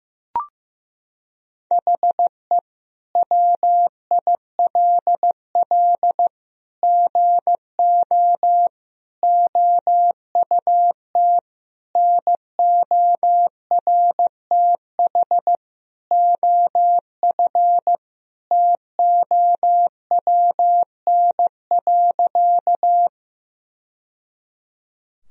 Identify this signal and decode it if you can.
Morse code, heard as a keyed tone, about 15 wpm, 700 Hz. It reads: HE WILL GO OUT NORTH OF TOWN.
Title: Sentences from Top 1000 Words - Review Part 1 - 15wpm